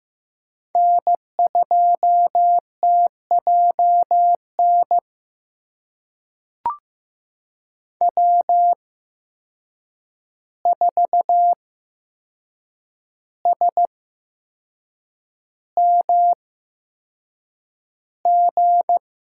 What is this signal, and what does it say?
Morse code, heard as a keyed tone, about 15 wpm, spaced as for 6 wpm, 700 Hz. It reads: N2TJN W4SMG
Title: USA Call Signs - 5 Characters - 8x Character Spacing - 15wpm